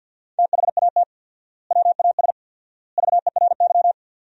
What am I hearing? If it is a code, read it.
Morse code, heard as a keyed tone, about 50 wpm, 700 Hz. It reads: THAT WAS VERY
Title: 4x Word-Spacing - Sentences from Top 200 Words - 50wpm